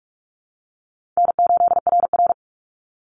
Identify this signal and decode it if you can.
Morse code, heard as a keyed tone, about 44 wpm, 700 Hz. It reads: N8RR
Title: Call Signs - 45wpm